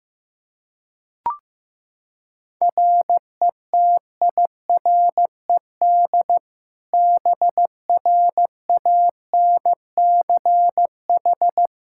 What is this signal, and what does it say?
Morse code, heard as a keyed tone, about 15 wpm, 700 Hz. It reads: RETIRED BRANCH
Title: Practice QSO - Exchange 3 - More Info - 15wpm